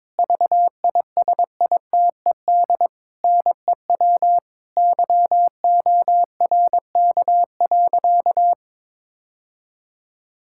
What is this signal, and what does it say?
Morse code, heard as a keyed tone, about 22 wpm, 700 Hz. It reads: VISITED NEW YORK.
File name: practice / 22wpm / Sentences with 2-Word Phrases - 22wpm